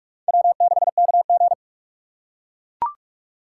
Morse code, heard as a keyed tone, about 39 wpm, 700 Hz. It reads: W6KG